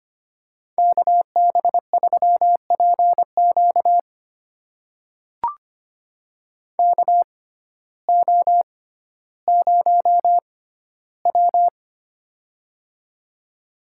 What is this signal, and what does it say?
Morse code, heard as a keyed tone, about 25 wpm, spaced as for 14 wpm, 700 Hz. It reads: KB3PQ KO0W
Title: USA Call Signs - 6x Character Spacing - 25wpm